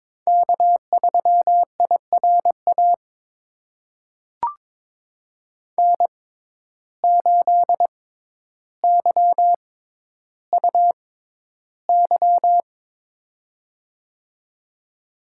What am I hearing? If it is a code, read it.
Morse code, heard as a keyed tone, about 22 wpm, spaced as for 11 wpm, 700 Hz. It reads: K3IRA N8YUY
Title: USA Call Signs - 5 Characters - 6x Character Spacing - 22wpm